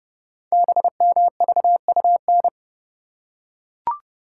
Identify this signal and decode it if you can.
Morse code, heard as a keyed tone, about 30 wpm, 700 Hz. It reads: BMVUN
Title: Code Groups - 5 Letters - 30wpm